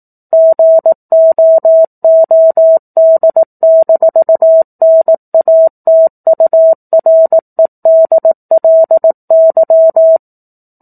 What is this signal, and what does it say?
Morse code, heard as a keyed tone, about 18 wpm, 650 Hz. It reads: GOOD-NATUREDLY
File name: 04-Chapter-04 18 WPM